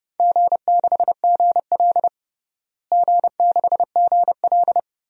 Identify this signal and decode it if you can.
Morse code, heard as a keyed tone, about 30 wpm, 700 Hz. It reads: G6GL G6GL
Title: DXCC-100 Call Signs - 4 Characters - Repeated 2x - 30wpm